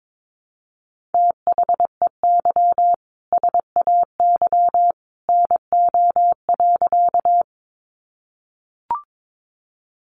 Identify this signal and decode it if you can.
Morse code, heard as a keyed tone, about 22 wpm, 700 Hz. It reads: THEY SAY NO.
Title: Easy Phrases 22wpm